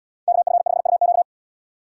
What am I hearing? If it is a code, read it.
Morse code, heard as a keyed tone, about 69 wpm, 700 Hz. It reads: KF5UQ